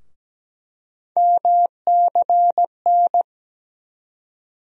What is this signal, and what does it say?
Morse code, heard as a keyed tone, about 17 wpm, 700 Hz. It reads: MCN
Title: USA Call Sign Postfixes - 3 Letters 17wpm